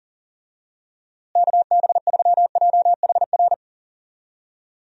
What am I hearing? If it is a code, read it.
Morse code, heard as a keyed tone, about 40 wpm, 700 Hz. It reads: KB3JHR
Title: USA Call Signs - 6 Characters - Repeated 2x - 40wpm